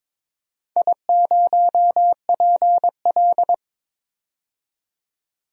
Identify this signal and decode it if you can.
Morse code, heard as a keyed tone, about 21 wpm, 700 Hz. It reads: I0PL
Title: DXCC-100 Call Signs - 4 Characters - 22wpm